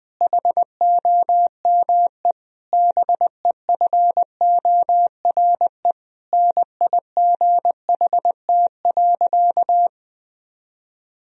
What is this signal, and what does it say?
Morse code, heard as a keyed tone, about 20 wpm, 700 Hz. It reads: HOME BEFORE NIGHT.
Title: Sentences from Top 1000 Words - Review Part 3 - 20wpm